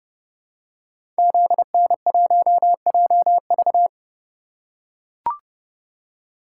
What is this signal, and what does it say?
Morse code, heard as a keyed tone, about 30 wpm, 700 Hz. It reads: ZN1JV
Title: Code Groups - 5 Letter:Numbers - 30wpm